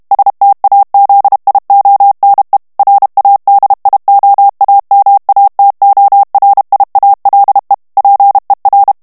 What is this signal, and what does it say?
Morse code, heard as a keyed tone, about 32 wpm, 800 Hz. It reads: STAZIONE RADIOAMATORIALE PER
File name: ITALIEN texte clair 1 vitesse 30 mots minute